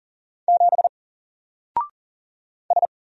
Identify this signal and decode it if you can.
Morse code, heard as a keyed tone, about 40 wpm, 700 Hz. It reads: 7 S
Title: Single Character (Letter:Number:Punctuation:Prosign) Alt 40wpm